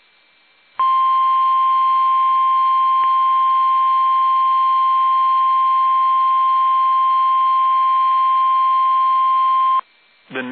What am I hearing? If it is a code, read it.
Emergency Alert System (SAME) tones (the 1050 Hz attention tone). It is heard from 0.7 to 9.8 s.
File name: eas / KUSO / KUSO